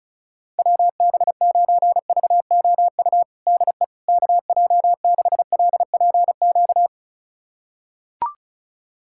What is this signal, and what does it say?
Morse code, heard as a keyed tone, about 36 wpm, 690 Hz. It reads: WB9VOU DE KJ6LPQ
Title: USA Callsign DE Callsign - 6 Characters - 35wpm